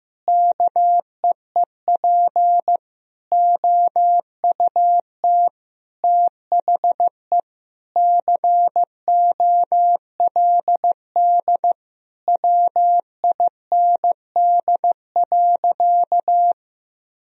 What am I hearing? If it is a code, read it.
Morse code, heard as a keyed tone, about 16 wpm, 700 Hz. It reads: KEEP OUT THE COLD WIND.
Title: Sentences from Top 1000 Words - Review Part 2 - 15wpm